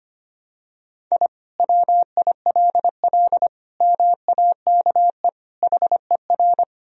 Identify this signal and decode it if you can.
Morse code, heard as a keyed tone, about 25 wpm, 700 Hz. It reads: I WILL MAKE HER
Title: Sentences from Top 1000 Words - Review Part 1 - 25wpm